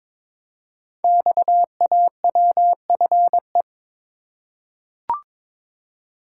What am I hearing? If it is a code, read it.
Morse code, heard as a keyed tone, about 22 wpm, 700 Hz. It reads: XAWFE
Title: Code Groups - 5 Letters - 22wpm